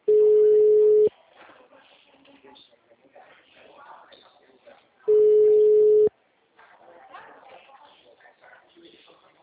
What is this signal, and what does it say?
Ringback tone (Europe, 425 Hz), with speech.